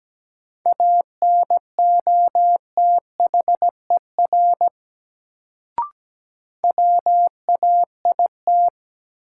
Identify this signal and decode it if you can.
Morse code, heard as a keyed tone, about 17 wpm, 700 Hz. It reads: ANOTHER WAIT
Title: Sets of 5 Words Encore - New Word Drill - 17wpm